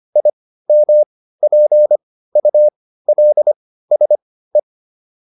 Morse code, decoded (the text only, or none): IMPULSE